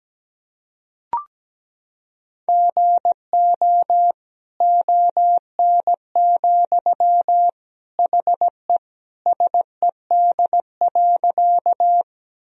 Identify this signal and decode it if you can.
Morse code, heard as a keyed tone, about 17 wpm, 700 Hz. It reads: GO ON, HE SED.
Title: Sentences from Top 100 Words and Abbreviations 17wpm